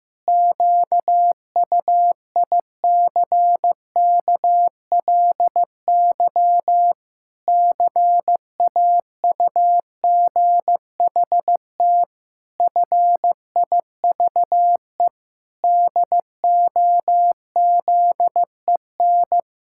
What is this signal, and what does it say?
Morse code, heard as a keyed tone, about 14 wpm, 700 Hz. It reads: QUICKLY CAUGHT FIVE DOZEN
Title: Pangram Challenge v2 - 15wpm